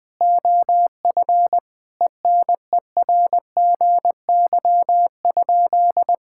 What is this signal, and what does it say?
Morse code, heard as a keyed tone, about 20 wpm, 700 Hz. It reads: OF ENERGY?
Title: Sentences from Top 1000 Words - Review Part 6 - 20wpm